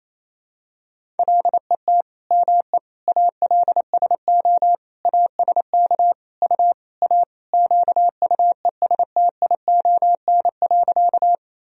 Morse code, heard as a keyed tone, about 27 wpm, 700 Hz. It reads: LET ME ALSO ASK U A QUESTION.
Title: Sentences from Top 400 Words and Abbreviations 28wpm